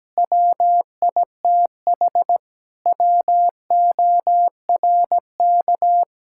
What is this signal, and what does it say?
Morse code, heard as a keyed tone, about 17 wpm, 700 Hz. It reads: WITH WORK
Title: Sentences from Top 1000 Words - Review Part 6 - 17wpm